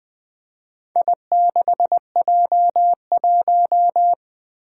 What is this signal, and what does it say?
Morse code, heard as a keyed tone, about 20 wpm, 700 Hz. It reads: I6J1
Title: Four Challenging Letter-Number Combinations 20wpm